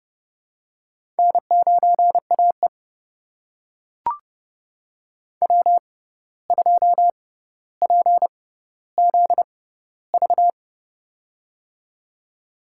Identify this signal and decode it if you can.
Morse code, heard as a keyed tone, about 30 wpm, spaced as for 14 wpm, 700 Hz. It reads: N9AE W2PZV
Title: USA Call Signs - 6x Character Spacing - 30wpm